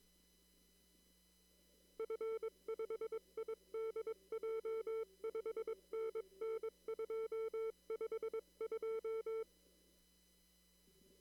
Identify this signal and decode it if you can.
Morse code, heard as a keyed tone, about 22 wpm, 450 Hz. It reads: F5IDJ 5NN 252
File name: CW / kt5j wpx cw-op-k5pi